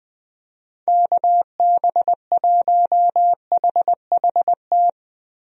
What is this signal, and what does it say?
Morse code, heard as a keyed tone, about 20 wpm, 700 Hz. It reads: KB1HHT
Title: USA Call Signs - 6 Characters - Repeated 3x - 20wpm